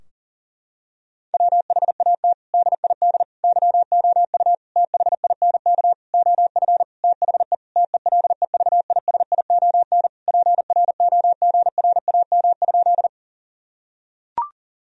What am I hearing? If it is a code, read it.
Morse code, heard as a keyed tone, about 37 wpm, 700 Hz. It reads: WHAT DID YOU THINK OF THE TELEVISION PROGRAM?
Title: Sentences from Top 1000 Words - Review Part 4 - 40wpm